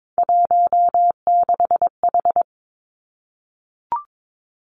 Morse code, heard as a keyed tone, about 22 wpm, 700 Hz. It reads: 16H
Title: Three Challenging Letter-Number Combinations 22wpm